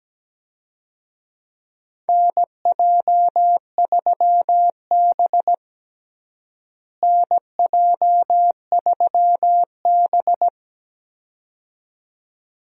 Morse code, heard as a keyed tone, about 17 wpm, 700 Hz. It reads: NJ3B NJ3B
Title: USA Call Signs - 4 Characters - Repeated 2x - 17wpm